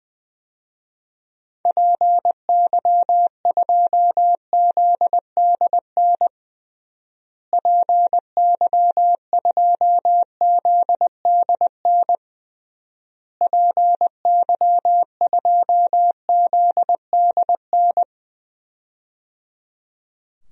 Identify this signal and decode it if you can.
Morse code, heard as a keyed tone, about 20 wpm, 700 Hz. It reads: PY2ZDN PY2ZDN PY2ZDN
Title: DXCC-100 Call Signs - 6 Characters - Repeated 3x - 20wpm